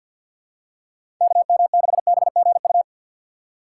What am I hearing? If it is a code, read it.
Morse code, heard as a keyed tone, about 52 wpm, 700 Hz. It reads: KM6BGU
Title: USA Call Signs - 6 Characters - 3x Character Spacing - 50wpm